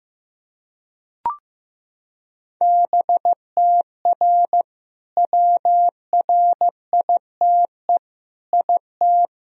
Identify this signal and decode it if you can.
Morse code, heard as a keyed tone, about 15 wpm, 700 Hz. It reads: BTR WRITE IT
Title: Sentences from Top 400 Words and Abbreviations 15wpm